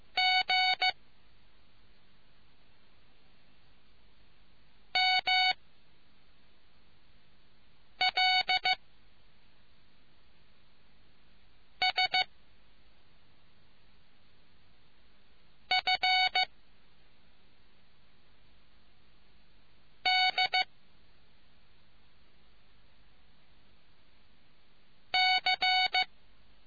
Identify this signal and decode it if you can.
Morse code, heard as a keyed tone, about 15 wpm, 740 Hz. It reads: G M L S F D C